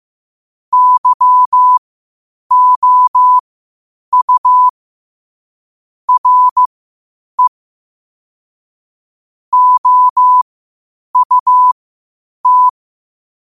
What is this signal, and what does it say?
Morse code, heard as a keyed tone, about 15 wpm, spaced as for 8 wpm, 1000 Hz. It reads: YOU RE OUT
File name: qotdfiles / 8wpm